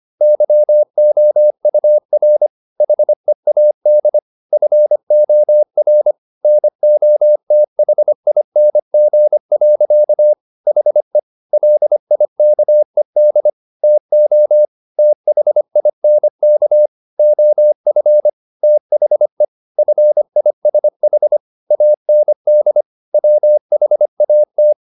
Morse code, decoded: YOUR HEAD FOR NOTHING. HE LIKED TO THINK OF THE FISH AND WHAT

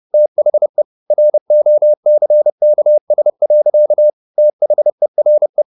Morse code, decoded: THE ROCKS. THERE

